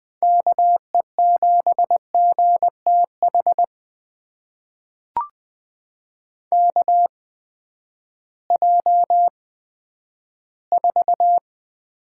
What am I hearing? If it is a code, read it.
Morse code, heard as a keyed tone, about 20 wpm, spaced as for 11 wpm, 700 Hz. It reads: KE7GTH KJ4
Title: USA Call Signs - 6 Characters - 8x Character Spacing - 20wpm